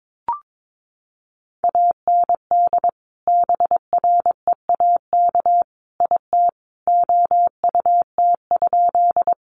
Morse code, heard as a keyed tone, about 22 wpm, 700 Hz. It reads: AND BREAK IT OUT?
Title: Sentences from Top 900 Words 22wpm